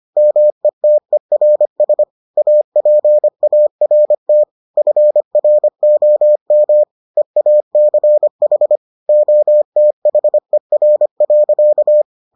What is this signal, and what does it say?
Morse code, heard as a keyed tone, about 25 wpm, 600 Hz. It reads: METERS APART FROM EACH OTHER.